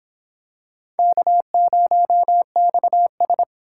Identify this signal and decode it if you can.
Morse code, heard as a keyed tone, about 25 wpm, 700 Hz. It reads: K0XS